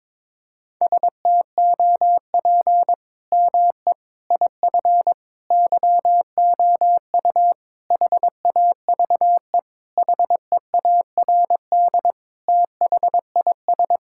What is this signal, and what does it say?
Morse code, heard as a keyed tone, about 22 wpm, 700 Hz. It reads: STOP ME IF YOU HAVE HEARD THIS
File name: practice / 22wpm / Sentences with 8-Word Phrases - 22wpm